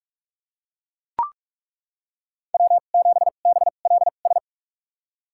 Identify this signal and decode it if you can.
Morse code, heard as a keyed tone, about 42 wpm, spaced as for 31 wpm, 700 Hz. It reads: W7BLS
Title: USA Call Signs - 5 Characters - 2x Character Spacing - 45wpm